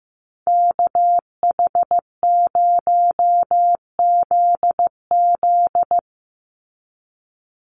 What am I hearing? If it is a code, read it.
Morse code, heard as a keyed tone, about 15 wpm, 700 Hz. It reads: KH0ZZ
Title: Call Signs Repeated 2x - 15wpm